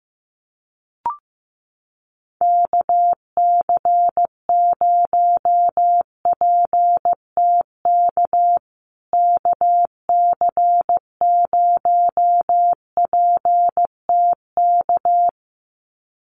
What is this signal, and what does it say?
Morse code, heard as a keyed tone, about 15 wpm, 700 Hz. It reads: KC0PTK KC0PTK
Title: USA Call Signs Repeated 2x 15wpm